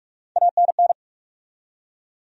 Morse code, heard as a keyed tone, about 44 wpm, 700 Hz. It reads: ANN
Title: USA Common Names - Max 5 Letters - 45wpm